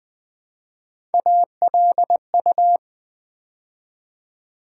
Morse code, heard as a keyed tone, about 20 wpm, 700 Hz. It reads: ALU
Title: Build A Word - 3 Letters At A Time - 8-Letter Words - 20wpm